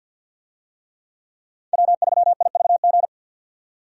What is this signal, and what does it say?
Morse code, heard as a keyed tone, about 53 wpm, 700 Hz. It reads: W3IVG
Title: USA Call Signs - 5 Characters - 50wpm